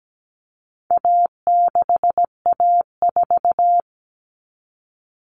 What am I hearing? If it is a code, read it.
Morse code, heard as a keyed tone, about 17 wpm, 700 Hz. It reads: A6A4
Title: Common Cut Numbers - 4-Digit 17wpm